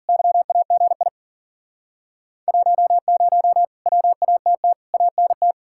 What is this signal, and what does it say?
Morse code, heard as a keyed tone, about 53 wpm, 700 Hz. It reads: YAGI 10 WATT ANT